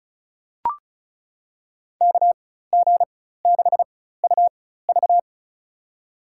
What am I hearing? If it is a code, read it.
Morse code, heard as a keyed tone, about 34 wpm, spaced as for 16 wpm, 700 Hz. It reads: KG6UV
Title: USA Call Signs - 5 Characters - 4x Character Spacing - 35wpm